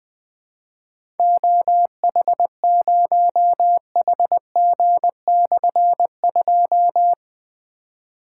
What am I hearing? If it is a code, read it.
Morse code, heard as a keyed tone, about 20 wpm, 700 Hz. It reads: OH0HG/2
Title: Difficult Call Signs 20wpm